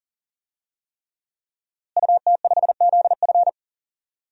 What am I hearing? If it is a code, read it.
Morse code, heard as a keyed tone, about 40 wpm, 700 Hz. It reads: UT5ZF
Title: DXCC-100 Call Signs - 5 Characters - Repeated 3x - 40wpm